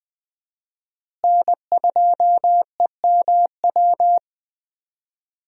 Morse code, heard as a keyed tone, about 21 wpm, 700 Hz. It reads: N2EMW
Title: USA Call Signs - 5 Characters - Repeated 2x - 20wpm